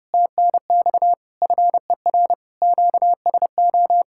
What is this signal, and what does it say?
Morse code, heard as a keyed tone, about 29 wpm, 700 Hz. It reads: TNX FER QSO